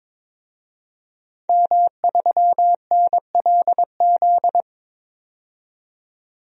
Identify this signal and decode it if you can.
Morse code, heard as a keyed tone, about 22 wpm, 700 Hz. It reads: M3NLZ
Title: DXCC-100 Call Signs - 5 Characters - 22wpm